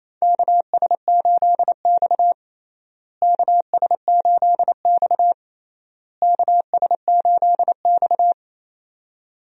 Morse code, heard as a keyed tone, about 28 wpm, 700 Hz. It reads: KS8X KS8X KS8X
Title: USA Call Signs - 4 Characters - Repeated 3x - 28wpm